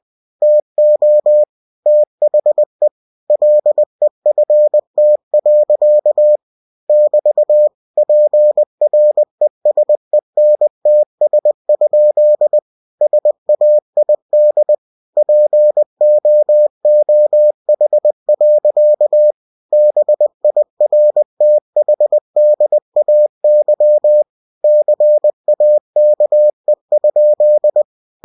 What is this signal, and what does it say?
Morse code, heard as a keyed tone, about 20 wpm, 600 Hz. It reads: TO THE LEFT. <BT> PRESENTS? SAID POOH. BIRTHDAY CAKE?
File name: cw-books / winnie-the-pooh / winnie-20wpm-page-0047